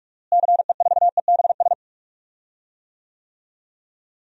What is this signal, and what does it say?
Morse code, heard as a keyed tone, about 47 wpm, 700 Hz. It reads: CE4EBS
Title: DXCC-100 Call Signs - 6 Characters - Repeated 2x - 45wpm